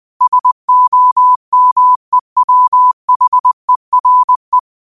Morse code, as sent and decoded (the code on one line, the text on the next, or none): ... --- -- . .-- .... .-. .
SOMEWHERE